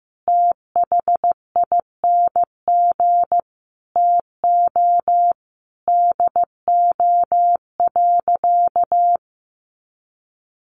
Morse code, heard as a keyed tone, about 15 wpm, 700 Hz. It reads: THING TO DO.
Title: Sentences from Top 100 Words 15wpm